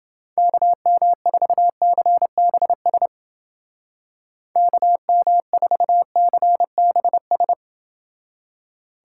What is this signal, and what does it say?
Morse code, heard as a keyed tone, about 28 wpm, 700 Hz. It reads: KM4CBS KM4CBS